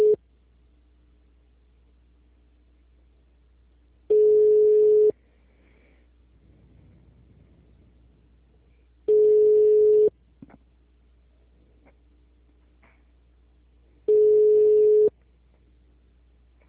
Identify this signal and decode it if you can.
Ringback tone (Europe, 425 Hz).